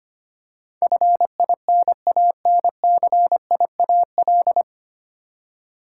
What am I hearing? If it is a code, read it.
Morse code, heard as a keyed tone, about 25 wpm, 690 Hz. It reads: FINANCIAL
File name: practice / 25wpm / 9 Letter Words - First 3 Letters Given - 25wpm